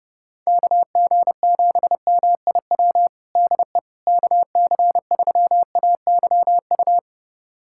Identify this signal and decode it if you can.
Morse code, heard as a keyed tone, about 30 wpm, 700 Hz. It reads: KG7MIW DE KC3AYU
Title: USA Callsign DE Callsign 30wpm